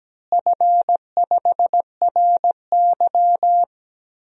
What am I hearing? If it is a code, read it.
Morse code, heard as a keyed tone, about 17 wpm, 700 Hz. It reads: F5RY
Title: DXCC-100 Call Signs - 4 Characters - Repeated 2x - 17wpm